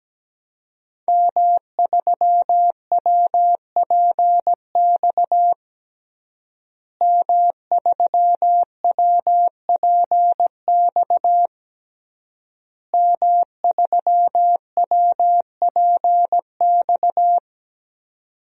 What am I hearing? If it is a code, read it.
Morse code, heard as a keyed tone, about 17 wpm, 700 Hz. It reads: M3WPX M3WPX M3WPX
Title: DXCC-100 Call Signs - 5 Characters - Repeated 3x - 17wpm